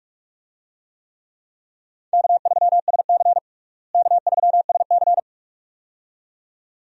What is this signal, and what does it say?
Morse code, heard as a keyed tone, about 43 wpm, 700 Hz. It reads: K3SC K3SC
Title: USA Call Signs - 4 Characters - Repeated 2x - 45wpm